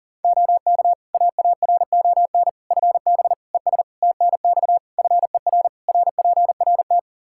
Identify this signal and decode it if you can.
Morse code, heard as a keyed tone, about 40 wpm, 700 Hz. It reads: OK AARON FB ES TNX FER RPRT
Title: K1USN SST QSO Practice 40wpm